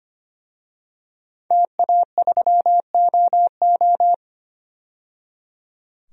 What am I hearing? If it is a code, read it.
Morse code, heard as a keyed tone, about 25 wpm, 700 Hz. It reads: TA3OO
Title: DXCC-100 Call Signs - 5 Characters - 25wpm